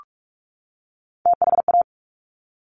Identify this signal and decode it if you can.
Morse code, heard as a keyed tone, about 46 wpm, 690 Hz. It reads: THA
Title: Top Three Letter Combinations in Words 45wpm